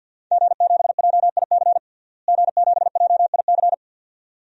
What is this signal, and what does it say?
Morse code, heard as a keyed tone, about 51 wpm, 700 Hz. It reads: G7JIC G7JIC
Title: DXCC-100 Call Signs - 5 Characters - Repeated 2x - 50wpm